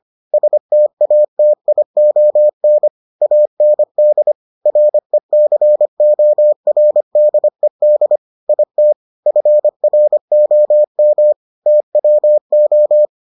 Morse code, heard as a keyed tone, about 25 wpm, 600 Hz. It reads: STATION AND RECORDED IT FROM TWO